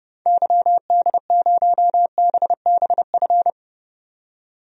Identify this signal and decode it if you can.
Morse code, heard as a keyed tone, about 30 wpm, 700 Hz. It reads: YD0BBF
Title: DXCC-100 Call Signs - 6 Characters - 30wpm